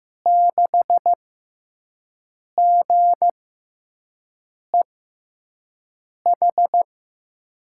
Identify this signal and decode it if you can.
Morse code, heard as a keyed tone, about 15 wpm, spaced as for 5 wpm, 700 Hz. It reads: 6GEH
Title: USA Call Signs - 5 Characters - 6x Character Spacing - 15wpm